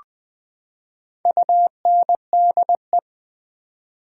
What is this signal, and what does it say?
Morse code, heard as a keyed tone, about 20 wpm, 700 Hz. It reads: UNDE
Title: Build A Word - 4 Letters At A Time - 11-Letter Words - 20wpm